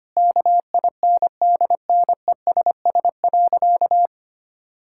Morse code, heard as a keyed tone, about 25 wpm, 700 Hz. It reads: KINDNESS.
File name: practice / 25wpm / Difficult Sentences with Hints - 25wpm